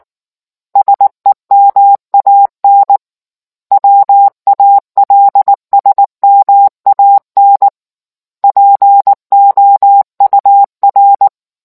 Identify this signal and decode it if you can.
Morse code, heard as a keyed tone, about 19 wpm, 800 Hz. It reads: SEMAN WALSMAN POUR